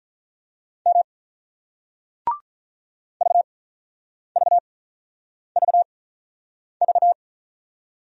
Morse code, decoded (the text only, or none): M VVVV